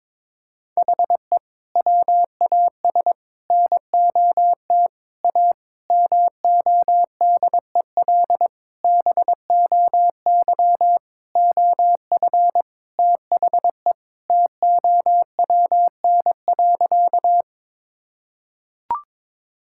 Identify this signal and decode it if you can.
Morse code, heard as a keyed tone, about 22 wpm, 700 Hz. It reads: HE WAS NOT A MODEL BOY OF THE TOWN.
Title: Sentences from Top 1000 Words - Review Part 5 - 22wpm